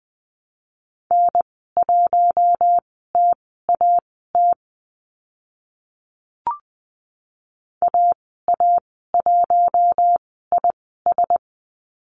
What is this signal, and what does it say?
Morse code, heard as a keyed tone, about 20 wpm, spaced as for 14 wpm, 700 Hz. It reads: N1TAT AA1IS
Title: 042 Lesson 8g - Number 1 - Call Signs 20wpm